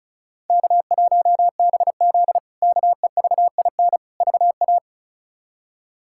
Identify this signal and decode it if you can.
Morse code, heard as a keyed tone, about 34 wpm, 700 Hz. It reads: K1BZ KEVIN VA
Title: K1USN SST QSO Practice - 35wpm